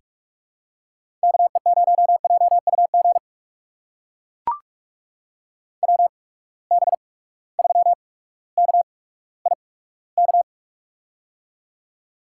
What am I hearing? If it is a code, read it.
Morse code, heard as a keyed tone, about 42 wpm, spaced as for 19 wpm, 700 Hz. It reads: KE0JUG WB3KIK
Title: USA Call Signs - 6 Characters - 8x Character Spacing - 45wpm